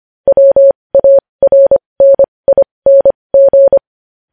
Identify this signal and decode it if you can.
Morse code, heard as a keyed tone, about 25 wpm, spaced as for 20 wpm, 550 Hz. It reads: WARNING
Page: News headlines in Morse code at 20 WPM